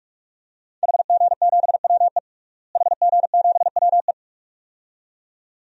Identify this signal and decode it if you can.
Morse code, heard as a keyed tone, about 46 wpm, 700 Hz. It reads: HG7WE HG7WE
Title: DXCC-100 Call Signs - 5 Characters - Repeated 2x - 45wpm